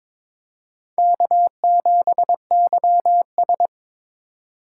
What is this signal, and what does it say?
Morse code, heard as a keyed tone, about 22 wpm, 700 Hz. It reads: K7YS
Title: USA Call Signs - 4 Characters - 4x Character Spacing - 22wpm